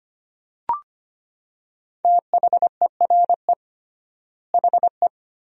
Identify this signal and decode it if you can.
Morse code, heard as a keyed tone, about 25 wpm, 700 Hz. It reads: THERE HE
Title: 3x Word-Spacing - Sentences from Top 100 Words - 25wpm